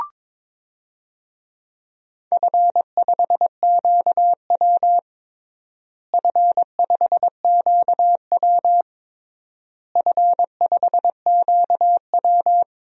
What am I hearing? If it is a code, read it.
Morse code, heard as a keyed tone, about 22 wpm, 700 Hz. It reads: F5QW F5QW F5QW
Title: DXCC-100 Call Signs - 4 Characters - Repeated 3x - 22wpm